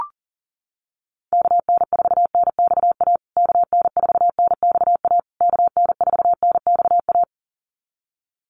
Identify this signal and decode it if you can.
Morse code, heard as a keyed tone, about 40 wpm, 700 Hz. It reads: KN4NXA KN4NXA KN4NXA
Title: USA Call Signs Repeated 3x 40wpm